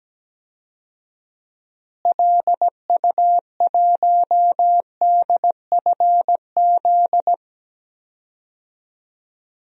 Morse code, heard as a keyed tone, about 17 wpm, 700 Hz. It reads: LU1DFZ
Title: DXCC-100 Call Signs - 6 Characters - Repeated 2x - 17wpm